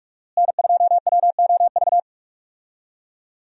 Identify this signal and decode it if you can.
Morse code, heard as a keyed tone, about 47 wpm, 700 Hz. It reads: N2WOV
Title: USA Call Signs - 5 Characters - Repeated 2x - 45wpm